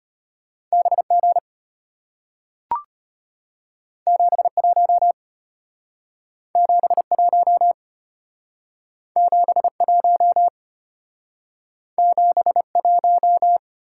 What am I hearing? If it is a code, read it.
Morse code, heard as a keyed tone, about 33 wpm, 700 Hz. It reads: BG 71 71 71 71